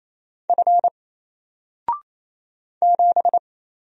Morse code, heard as a keyed tone, about 27 wpm, 700 Hz. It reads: F 7
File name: Single-Letter-Number-28wpm